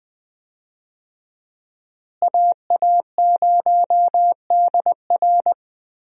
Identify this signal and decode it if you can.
Morse code, heard as a keyed tone, about 20 wpm, 700 Hz. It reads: AA0DR
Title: USA Call Signs - 5 Characters - Repeated 2x - 20wpm